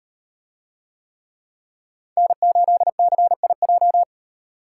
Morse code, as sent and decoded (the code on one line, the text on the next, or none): -. ---.. -.-. .. .---
N8CIJ